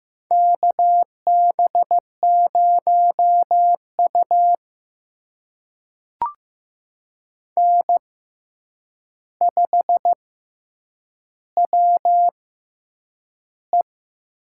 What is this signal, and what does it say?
Morse code, heard as a keyed tone, about 15 wpm, spaced as for 8 wpm, 700 Hz. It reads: KB0U N5WE
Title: USA Call Signs - 4 Characters - 6x Character Spacing - 15wpm